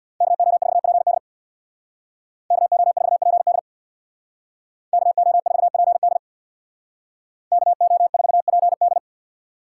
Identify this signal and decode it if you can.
Morse code, heard as a keyed tone, about 60 wpm, 700 Hz. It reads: KO4PD KO4PD KO4PD KO4PD
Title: Speed Racing - USA 2x2 Call Signs 50wpm